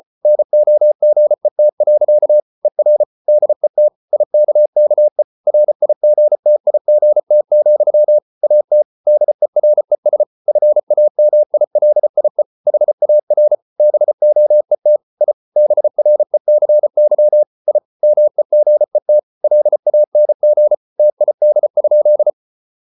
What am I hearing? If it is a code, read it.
Morse code, heard as a keyed tone, about 34 wpm, 600 Hz. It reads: NOGET. ER DET IKKE RIGTIGT, AT DERES FAMILIE HAR BOET I BRECY I MEGET LANG TID?
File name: Ferie QRQ 34wpm